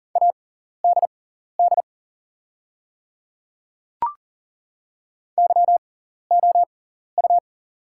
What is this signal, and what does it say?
Morse code, heard as a keyed tone, about 38 wpm, spaced as for 14 wpm, 690 Hz. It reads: ADD YOU
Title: 6x Character Spacing - Top 200 Words - 40wpm